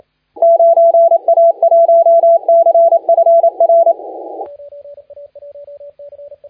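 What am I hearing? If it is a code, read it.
Morse code, heard as a keyed tone, about 28 wpm, 670 Hz, with radio noise. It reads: 9A1CFR
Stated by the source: Morse contest audio recorded by LA4C during the SAC CW 2018 contest.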